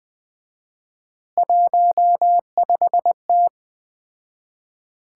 Morse code, heard as a keyed tone, about 20 wpm, 700 Hz. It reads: 15T
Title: Common QSO Phrases - 4x Word-Spacing - 20wpm